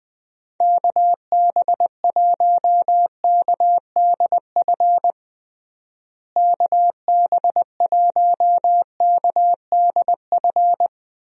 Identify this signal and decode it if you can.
Morse code, heard as a keyed tone, about 20 wpm, 700 Hz. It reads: KB1KDF KB1KDF
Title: USA Call Signs - 6 Characters - Repeated 3x - 20wpm